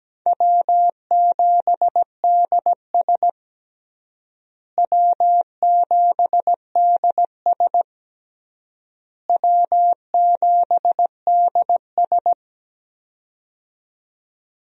Morse code, heard as a keyed tone, about 17 wpm, 700 Hz. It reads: W7DS W7DS W7DS
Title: USA Call Signs - 4 Characters - Repeated 3x - 17wpm